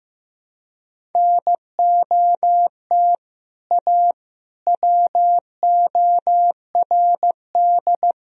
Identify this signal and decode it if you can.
Morse code, heard as a keyed tone, about 15 wpm, 700 Hz. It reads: NOT A WORD
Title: Sentences from Top 1000 Words - Review Part 1 - 15wpm